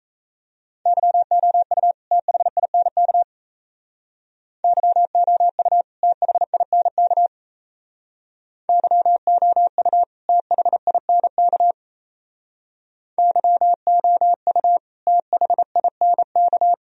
Morse code, heard as a keyed tone, about 34 wpm, 700 Hz. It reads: YOU THINK YOU THINK YOU THINK YOU THINK
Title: Speed-Racing - Sets of 2 Words 28wpm